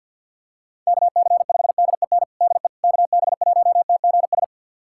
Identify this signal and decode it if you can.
Morse code, heard as a keyed tone, about 52 wpm, 690 Hz. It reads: KC5DEN DE KB1TGS
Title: USA Callsign DE Callsign - 6 Characters - 50wpm